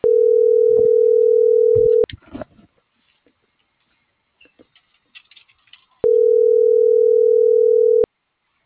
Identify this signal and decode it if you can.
Ringback tone (North America), 440 + 480 Hz.